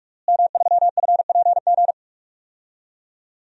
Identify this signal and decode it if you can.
Morse code, heard as a keyed tone, about 46 wpm, 700 Hz. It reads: M3FPG
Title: DXCC-100 Call Signs - 5 Characters - Repeated 3x - 45wpm